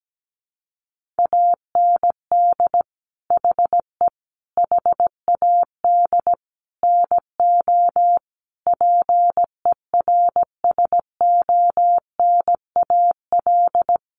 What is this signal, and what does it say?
Morse code, heard as a keyed tone, about 17 wpm, 700 Hz. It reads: AND HE HAD NO PERSONAL
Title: Sentences from Top 1400 Words 17wpm